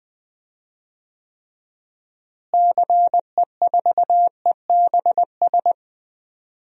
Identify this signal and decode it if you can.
Morse code, heard as a keyed tone, about 20 wpm, 700 Hz. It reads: CE4EBS
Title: DXCC-100 Call Signs - 6 Characters - Repeated 2x - 20wpm